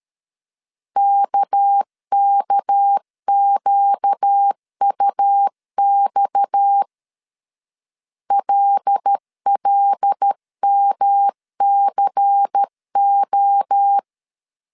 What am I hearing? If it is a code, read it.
Morse code, heard as a keyed tone, about 13 wpm, 800 Hz. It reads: KKQUX LLMCO
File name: CW / DET / 1 6 1500C S C 12wpm 3533